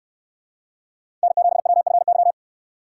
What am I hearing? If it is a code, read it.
Morse code, heard as a keyed tone, about 68 wpm, 700 Hz. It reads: N7WFY